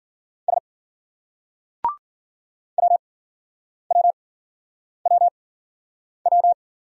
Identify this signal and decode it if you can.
Morse code, heard as a keyed tone, about 50 wpm, spaced as for 9 wpm, 700 Hz. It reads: S WWWW